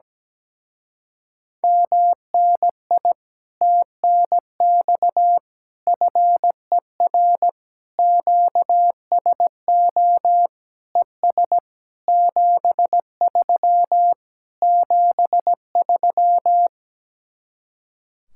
Morse code, heard as a keyed tone, about 17 wpm, 700 Hz. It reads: MNI TNX FER QSO ES 73 73
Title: Speed-Racing Common QSO Phrases - 17wpm